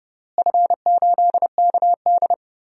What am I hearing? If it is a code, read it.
Morse code, heard as a keyed tone, about 30 wpm, 700 Hz. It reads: F8KD
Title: DXCC-100 Call Signs - 4 Characters - Repeated 2x - 30wpm